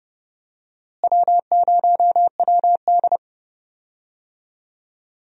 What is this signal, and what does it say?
Morse code, heard as a keyed tone, about 30 wpm, 700 Hz. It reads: W0WD
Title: USA Call Signs - 4 Characters - Repeated 2x - 30wpm